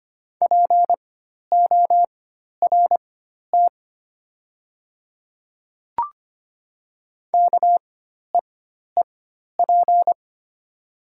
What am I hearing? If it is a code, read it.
Morse code, heard as a keyed tone, about 25 wpm, spaced as for 12 wpm, 690 Hz. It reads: PORT KEEP
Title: 4x Character Spacing - Top 500 Words - 25wpm